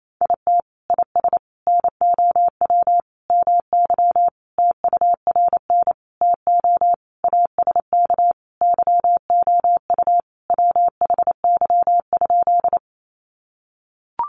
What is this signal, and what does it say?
Morse code, heard as a keyed tone, about 28 wpm, 700 Hz. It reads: IT IS NOW MY TURN TO ASK YOU WHY?
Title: Sentences from Top 200 Words 28wpm